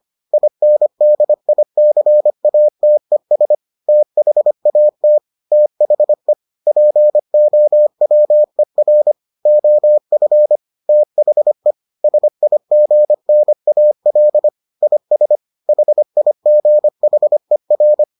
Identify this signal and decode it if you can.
Morse code, heard as a keyed tone, about 25 wpm, 600 Hz. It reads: INDICATES THAT THE POWER OF THE SIGNAL IS HIGHER